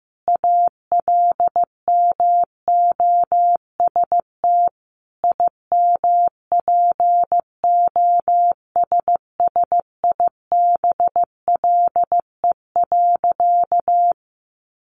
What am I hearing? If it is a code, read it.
Morse code, heard as a keyed tone, about 15 wpm, 700 Hz. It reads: ALMOST IMPOSSIBLE.
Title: Sentences from Top 1100 Words 15wpm